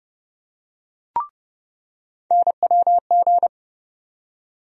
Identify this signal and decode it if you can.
Morse code, heard as a keyed tone, about 31 wpm, 700 Hz. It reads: NWG
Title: USA Call Sign Postfixes - 3 Letters 30wpm